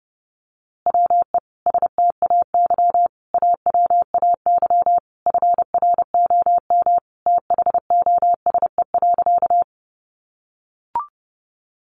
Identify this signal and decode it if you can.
Morse code, heard as a keyed tone, about 30 wpm, 700 Hz. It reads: WE STAY AWAY FROM THOSE.
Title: A Sound Of Thunder 30wpm